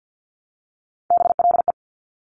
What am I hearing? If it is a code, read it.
Morse code, heard as a keyed tone, about 50 wpm, 700 Hz. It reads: BLE
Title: Top Three Letter Combinations in Words 50wpm